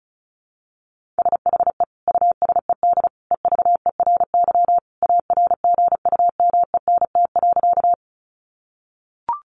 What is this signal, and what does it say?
Morse code, heard as a keyed tone, about 34 wpm, 700 Hz. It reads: SHE USED EVERY ARGUMENT.
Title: Sentences from Top 1400 Words 35wpm